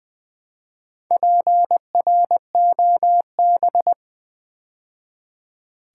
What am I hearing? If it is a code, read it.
Morse code, heard as a keyed tone, about 20 wpm, 700 Hz. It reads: PROB
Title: Build A Word - 2 Letters At A Time - 8-Letter Words - 20wpm